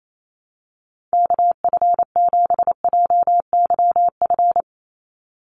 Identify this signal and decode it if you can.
Morse code, heard as a keyed tone, about 28 wpm, 700 Hz. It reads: KF7JYF
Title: USA Call Signs 28wpm